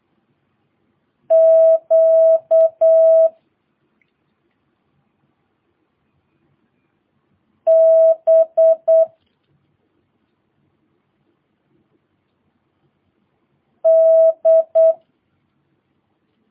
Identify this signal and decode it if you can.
Morse code, heard as a keyed tone, about 8 wpm, 650 Hz. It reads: Q B D
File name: jh8cbh / 2022cw